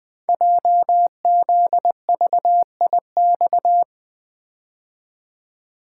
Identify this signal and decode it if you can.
Morse code, heard as a keyed tone, about 20 wpm, 700 Hz. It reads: JZVIX
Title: Code Groups - 5 Letters - 20wpm